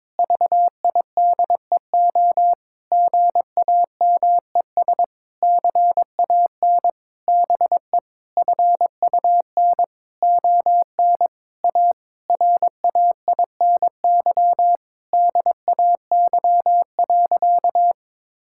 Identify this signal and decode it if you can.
Morse code, heard as a keyed tone, about 22 wpm, 700 Hz. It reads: VIDEO GAMES CAN BE FUN ON A RAINY DAY.
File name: Difficult Sentences with Hints - 22wpm